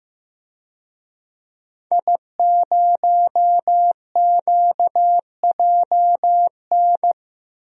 Morse code, heard as a keyed tone, about 15 wpm, 700 Hz. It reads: I0QJN
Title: DXCC-100 Call Signs - 5 Characters - Repeated 3x - 15wpm